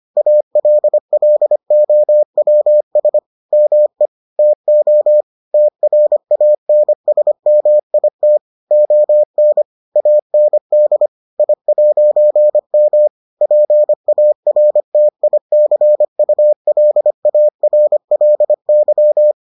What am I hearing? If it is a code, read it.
Morse code, heard as a keyed tone, about 25 wpm, 600 Hz. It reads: ALLOWS ME TO TRANSMIT ON AND I'M PARTICULARLY